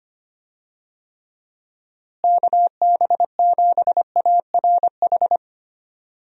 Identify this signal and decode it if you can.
Morse code, heard as a keyed tone, about 25 wpm, 700 Hz. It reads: KB7ARH
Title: USA Call Signs - 6 Characters - Repeated 3x - 25wpm